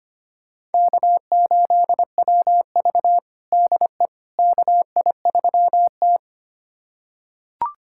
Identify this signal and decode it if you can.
Morse code, heard as a keyed tone, about 25 wpm, 700 Hz. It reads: K8WV DE KI3T
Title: USA Callsign DE Callsign - 4 Characters - 25wpm